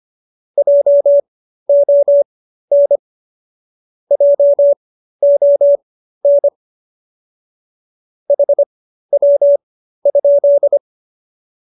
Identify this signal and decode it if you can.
Morse code, heard as a keyed tone, about 25 wpm, spaced as for 13 wpm, 570 Hz. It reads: JON JON HW?